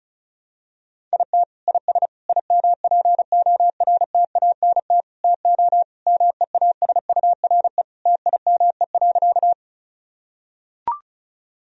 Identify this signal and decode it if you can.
Morse code, heard as a keyed tone, about 36 wpm, 700 Hz. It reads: IT IS IMPORTANT TO MEASURE TIME.
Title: Sentences from Top 1000 Words - Review Part 2 - 35wpm